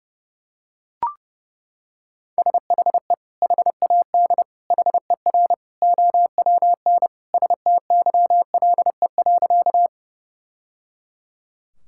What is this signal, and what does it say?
Morse code, heard as a keyed tone, about 30 wpm, 700 Hz. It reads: SHE HAD HER OWN STYLE.
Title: Sentences from Top 1000 Words - Review Part 5 - 30wpm